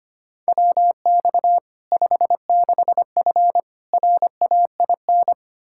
Morse code, heard as a keyed tone, about 25 wpm, 700 Hz. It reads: WX 56F RAIN
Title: Practice QSO - Exchange 2 - Info - 25wpm